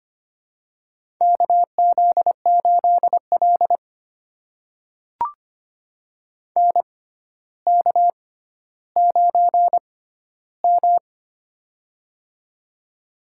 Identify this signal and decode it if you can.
Morse code, heard as a keyed tone, about 25 wpm, spaced as for 13 wpm, 700 Hz. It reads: KZ8L NK9M
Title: USA Call Signs - 4 Characters - 6x Character Spacing - 25wpm